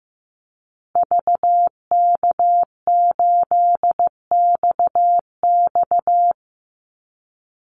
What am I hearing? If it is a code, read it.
Morse code, heard as a keyed tone, about 15 wpm, 700 Hz. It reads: VK8XX
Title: Call Signs Repeated 2x - 15wpm